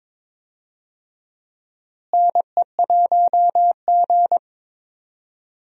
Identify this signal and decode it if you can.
Morse code, heard as a keyed tone, about 21 wpm, 700 Hz. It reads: NE1G
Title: USA Call Signs - 4 Characters - Repeated 2x - 22wpm